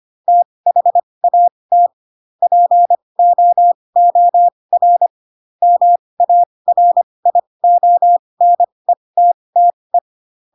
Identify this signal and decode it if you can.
Morse code, heard as a keyed tone, about 25 wpm, spaced as for 20 wpm, 700 Hz. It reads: THAT POOR MARIONETTE